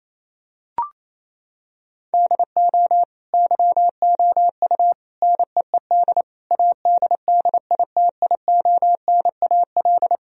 Morse code, heard as a keyed tone, about 28 wpm, 700 Hz. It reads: DO YOU NEED ADDITIONAL
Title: Sentences from Top 1000 Words - Review Part 4 - 28wpm